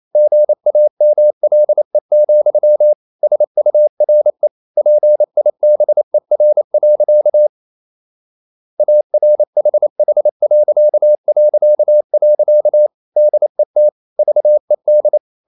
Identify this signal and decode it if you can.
Morse code, heard as a keyed tone, about 28 wpm, 600 Hz. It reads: GAMLE, SURE PIBER. ARHH... DET VED
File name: Ferie QRQ 28wpm